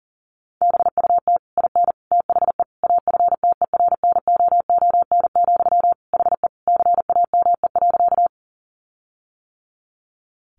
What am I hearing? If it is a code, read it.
Morse code, heard as a keyed tone, about 40 wpm, 700 Hz. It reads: BUT IN THE AFTERNOON, HE CAME.
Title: Sentences from Top 1500 Words 40wpm